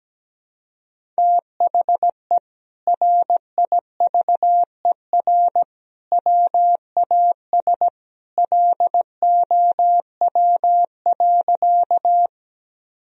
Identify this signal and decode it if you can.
Morse code, heard as a keyed tone, about 17 wpm, 700 Hz. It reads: THE RIVER WAS LOW.
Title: Sentences from Top 1000 Words - Review Part 1 - 17wpm